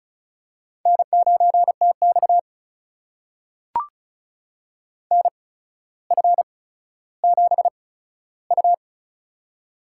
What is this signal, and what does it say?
Morse code, heard as a keyed tone, about 36 wpm, spaced as for 15 wpm, 700 Hz. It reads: N9TX NF7U
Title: USA Call Signs - 4 Characters - 8x Character Spacing - 35wpm